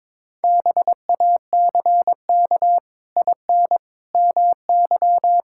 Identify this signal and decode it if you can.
Morse code, heard as a keyed tone, about 21 wpm, 700 Hz. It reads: BACK IN MY
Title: Pangram Challenge v2 - 22wpm